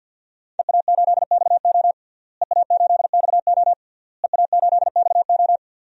Morse code, heard as a keyed tone, about 48 wpm, 700 Hz. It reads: EA8XO EA8XO EA8XO
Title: DXCC-100 Call Signs - 5 Characters - Repeated 3x - 50wpm